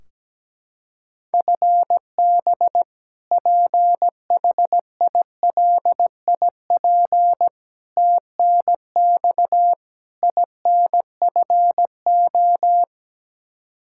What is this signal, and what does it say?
Morse code, heard as a keyed tone, about 17 wpm, 700 Hz. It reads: FB PHILIP TNX INFO
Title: Practice QSO - Exchange 3 - More Info - 17wpm